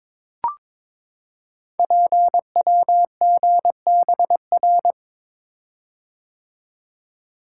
Morse code, heard as a keyed tone, about 22 wpm, 700 Hz. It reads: PWGBR
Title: Code Groups - 5 Letters - 22wpm